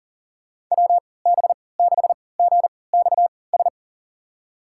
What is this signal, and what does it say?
Morse code, heard as a keyed tone, about 40 wpm, spaced as for 23 wpm, 700 Hz. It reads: WB6GXS
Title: USA Call Signs - 6 Characters - 3x Character Spacing - 40wpm